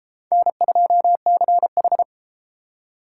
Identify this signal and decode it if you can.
Morse code, heard as a keyed tone, about 32 wpm, 700 Hz. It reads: N2CH